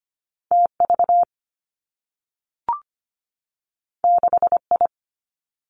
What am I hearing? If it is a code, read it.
Morse code, heard as a keyed tone, about 25 wpm, 700 Hz. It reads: TV 6I
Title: Two Challenging Letter-Number Combinations 25wpm